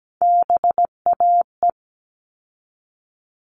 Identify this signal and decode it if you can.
Morse code, heard as a keyed tone, about 17 wpm, 700 Hz. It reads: BAE
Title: All Cut Numbers - 3-Digit 17wpm